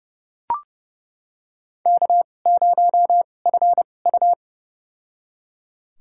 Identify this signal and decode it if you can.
Morse code, heard as a keyed tone, about 30 wpm, spaced as for 22 wpm, 700 Hz. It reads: K0FU
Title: USA Call Signs - 4 Characters - 2x Character Spacing - 30wpm